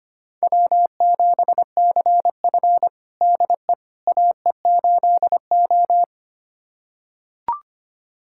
Morse code, heard as a keyed tone, about 25 wpm, 700 Hz. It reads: W7CF DE AE8O
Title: USA Callsign DE Callsign - 4 Characters - 25wpm